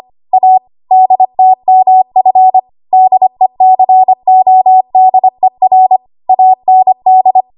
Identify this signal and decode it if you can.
Morse code, heard as a keyed tone, about 25 wpm, 750 Hz. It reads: A DTMF DECODER AND
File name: Morse / 25WPM